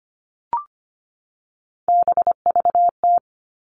Morse code, heard as a keyed tone, about 25 wpm, 700 Hz. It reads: BVT
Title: All Cut Numbers - 3-Digit 25wpm